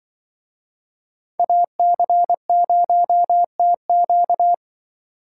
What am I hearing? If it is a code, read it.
Morse code, heard as a keyed tone, about 25 wpm, 700 Hz. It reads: AC0TQ
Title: Speed Racing - USA 2x2 Call Signs 20wpm